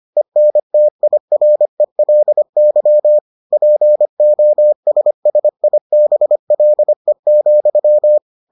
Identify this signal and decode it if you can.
Morse code, heard as a keyed tone, about 25 wpm, 600 Hz. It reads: ENTIRELY POSSIBLE,